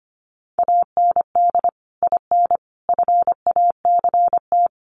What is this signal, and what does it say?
Morse code, heard as a keyed tone, about 25 wpm, 700 Hz. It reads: AND IN FACT
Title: Sentences from Top 1000 Words 25wpm